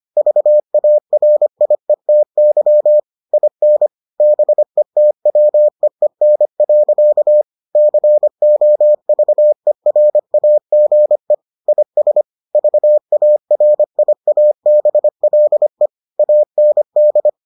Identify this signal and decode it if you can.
Morse code, heard as a keyed tone, about 25 wpm, 600 Hz. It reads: VARIETY IN BETWEEN. COVERAGE IS VARIABLE AND